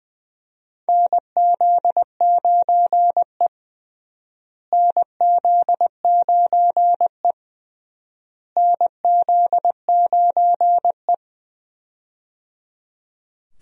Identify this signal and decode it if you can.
Morse code, heard as a keyed tone, about 20 wpm, 700 Hz. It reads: NZ9E NZ9E NZ9E
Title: USA Call Signs - 4 Characters - Repeated 3x - 20wpm